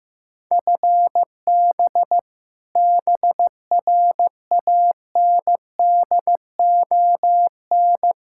Morse code, heard as a keyed tone, about 15 wpm, 700 Hz. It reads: FB BRANDON
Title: Practice QSO - Exchange 3 - More Info - 15wpm